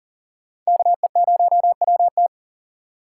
Morse code, heard as a keyed tone, about 54 wpm, spaced as for 47 wpm, 700 Hz. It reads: KE0WT
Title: Speed Racing - USA 2x2 Call Signs 40wpm